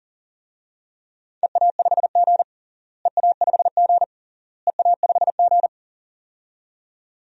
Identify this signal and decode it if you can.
Morse code, heard as a keyed tone, about 40 wpm, 700 Hz. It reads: EA5G EA5G EA5G
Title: DXCC-100 Call Signs - 4 Characters - Repeated 3x - 40wpm